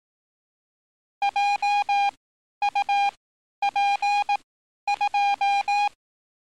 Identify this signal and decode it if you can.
Morse code, heard as a keyed tone, about 18 wpm, spaced as for 11 wpm, 810 Hz. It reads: JUP2
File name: training / NAVPERS 11049to11060 / NAVPERS 11051 RA RB